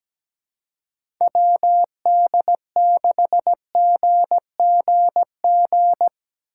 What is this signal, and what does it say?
Morse code, heard as a keyed tone, about 17 wpm, 700 Hz. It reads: WD6GGG
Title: USA Call Signs - 6 Characters - 2x Character Spacing - 17wpm